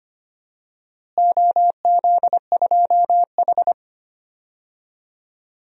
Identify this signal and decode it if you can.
Morse code, heard as a keyed tone, about 25 wpm, 700 Hz. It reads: OZ2H
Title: DXCC-100 Call Signs - 4 Characters - Repeated 3x - 25wpm